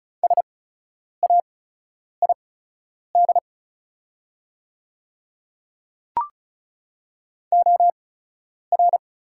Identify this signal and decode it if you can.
Morse code, heard as a keyed tone, about 36 wpm, spaced as for 10 wpm, 700 Hz. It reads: SAID OR
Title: 8x Character Spacing - Top 100 Words - 35wpm